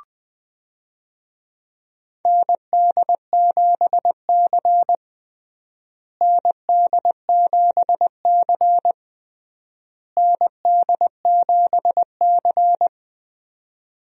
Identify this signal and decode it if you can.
Morse code, heard as a keyed tone, about 20 wpm, 700 Hz. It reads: ND7C ND7C ND7C
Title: USA Call Signs - 4 Characters - Repeated 3x - 20wpm